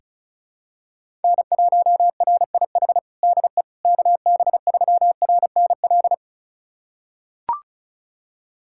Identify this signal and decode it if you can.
Morse code, heard as a keyed tone, about 34 wpm, 700 Hz. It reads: N1RIH DE KB3RNL
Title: USA Callsign DE Callsign 35wpm